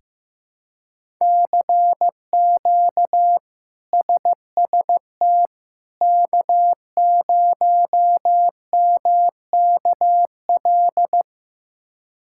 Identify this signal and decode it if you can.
Morse code, heard as a keyed tone, about 15 wpm, 700 Hz. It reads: CQ SST K0MKL
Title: K1USN SST QSO Practice - 15wpm